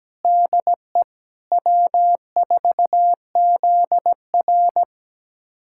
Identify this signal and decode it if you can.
Morse code, heard as a keyed tone, about 17 wpm, 700 Hz. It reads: DE W4ZR
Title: USA Callsign DE Callsign - 4 Characters - 17wpm